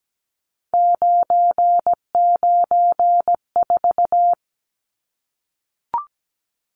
Morse code, heard as a keyed tone, about 17 wpm, 700 Hz. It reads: 994
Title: Numbers - 3-Digit 17wpm